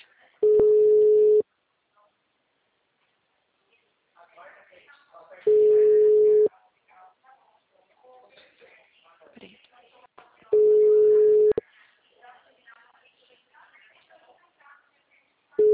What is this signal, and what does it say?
Ringback tone (Europe, 425 Hz), with speech.
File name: out- -2049- - - .12925